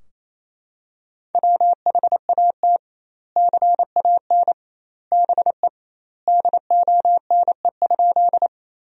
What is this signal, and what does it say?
Morse code, heard as a keyed tone, about 28 wpm, 700 Hz. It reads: WHAT CAN BE DONE?
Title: 2x Word-Spacing - Sentences from Top 500 Words - 28wpm